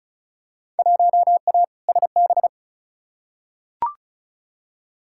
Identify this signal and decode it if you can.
Morse code, heard as a keyed tone, about 34 wpm, 700 Hz. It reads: 1A SB
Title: Field Day Exchanges Encore - 35wpm